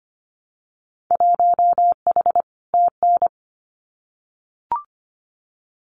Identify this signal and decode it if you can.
Morse code, heard as a keyed tone, about 25 wpm, 700 Hz. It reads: 1H TN
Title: Winter Field Day Exchanges 25wpm